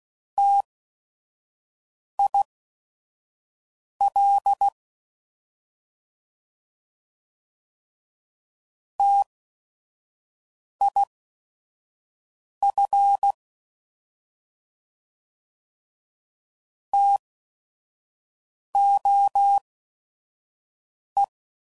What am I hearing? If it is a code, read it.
Morse code, heard as a keyed tone, about 16 wpm, spaced as for 5 wpm, 800 Hz. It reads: TIL TIF TOE